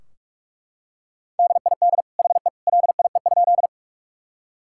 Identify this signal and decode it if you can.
Morse code, heard as a keyed tone, about 41 wpm, 700 Hz. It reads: DID HE LIE?
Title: Sentences from Top 1600 Words 45wpm